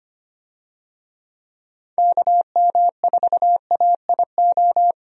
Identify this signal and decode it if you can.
Morse code, heard as a keyed tone, about 25 wpm, 700 Hz. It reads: KM4AIO
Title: USA Call Signs - 6 Characters - 25wpm